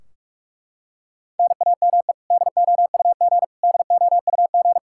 Morse code, heard as a keyed tone, about 41 wpm, 700 Hz. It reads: NAME DOUG DOUG
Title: Practice QSO - Exchange 1 - Beginning - 45wpm